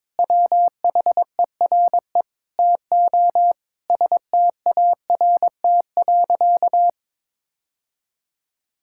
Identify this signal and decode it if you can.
Morse code, heard as a keyed tone, about 21 wpm, 700 Hz. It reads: WHERE TO START.
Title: Sentences from Top 1000 Words - Review Part 5 - 22wpm